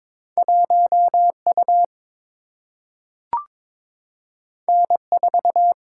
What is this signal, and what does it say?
Morse code, heard as a keyed tone, about 22 wpm, 700 Hz. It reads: 1U N4
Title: Single-Letter Single-Number 22wpm